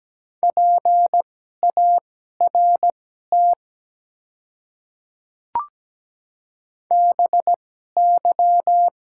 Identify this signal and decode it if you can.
Morse code, heard as a keyed tone, about 17 wpm, spaced as for 12 wpm, 690 Hz. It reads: PART BY
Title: 2x Character Spacing - Top 300 Words - 17wpm